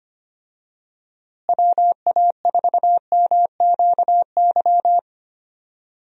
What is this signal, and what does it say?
Morse code, heard as a keyed tone, about 25 wpm, 700 Hz. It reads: WA4MQY
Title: USA Call Signs - 6 Characters - Repeated 3x - 25wpm